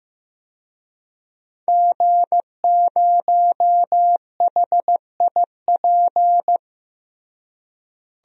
Morse code, heard as a keyed tone, about 15 wpm, 700 Hz. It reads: G0HIP
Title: DXCC-100 Call Signs - 5 Characters - Repeated 3x - 15wpm